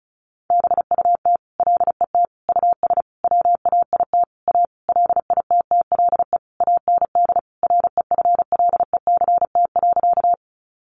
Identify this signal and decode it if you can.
Morse code, heard as a keyed tone, about 34 wpm, 700 Hz. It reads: BUT LET US WAIT A LITTLE AND REFLECT.
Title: Sentences from Top 1500 Words 35wpm